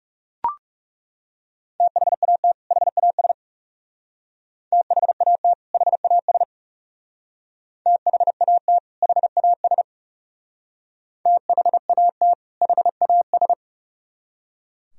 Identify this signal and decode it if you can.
Morse code, heard as a keyed tone, about 35 wpm, 700 Hz. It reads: THAT HAS THAT HAS THAT HAS THAT HAS
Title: Speed-Racing - Sets of 2 Words 30wpm